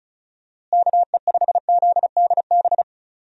Morse code, heard as a keyed tone, about 34 wpm, 700 Hz. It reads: KE5ZDB
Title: USA Call Signs - 6 Characters - Repeated 2x - 35wpm